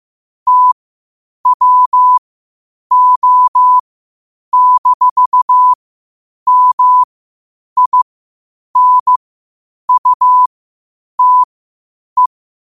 Morse code, heard as a keyed tone, about 15 wpm, spaced as for 8 wpm, 1000 Hz. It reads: TWO-MINUTE
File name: qotdfiles / 8wpm